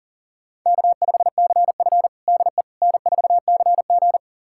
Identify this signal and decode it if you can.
Morse code, heard as a keyed tone, about 40 wpm, 700 Hz. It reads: K5CF DE N4CG
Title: USA Callsign DE Callsign - 4 Characters - 40wpm